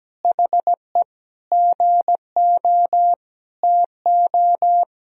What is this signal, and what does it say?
Morse code, heard as a keyed tone, about 17 wpm, 700 Hz. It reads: HE GO TO
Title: Sentences from Top 1000 Words - Review Part 1 - 17wpm